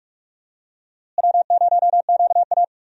Morse code, heard as a keyed tone, about 47 wpm, 700 Hz. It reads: W0QA
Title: USA Call Signs - 4 Characters - Repeated 3x - 45wpm